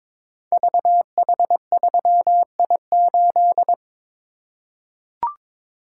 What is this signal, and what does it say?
Morse code, heard as a keyed tone, about 22 wpm, 700 Hz. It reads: VH3I8
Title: Code Groups - 5 Letter:Numbers - 22wpm